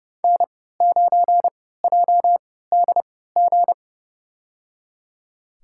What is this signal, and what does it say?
Morse code, heard as a keyed tone, about 32 wpm, spaced as for 17 wpm, 700 Hz. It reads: N9JDG
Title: USA Call Signs - 3x Character Spacing - 30wpm